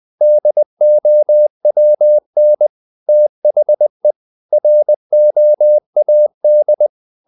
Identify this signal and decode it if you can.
Morse code, heard as a keyed tone, about 20 wpm, 600 Hz. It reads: DOWN THE ROAD
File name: the old man and the sea part 2 20wpm